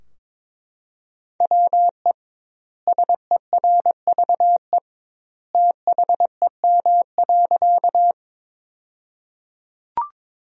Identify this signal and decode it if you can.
Morse code, heard as a keyed tone, about 21 wpm, 700 Hz. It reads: WE SERVE THEM.
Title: 2x Word-Spacing - Sentences from Top 400 Words - 22wpm